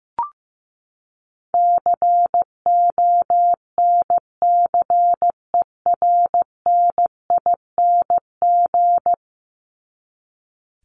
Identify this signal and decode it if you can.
Morse code, heard as a keyed tone, about 15 wpm, 700 Hz. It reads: CONCERNING
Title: ING Suffix - Top 100 Words with Base Words 15wpm